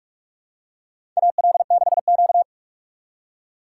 Morse code, heard as a keyed tone, about 47 wpm, 700 Hz. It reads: AL6Q
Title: USA Call Signs - 4 Characters - 4x Character Spacing - 45wpm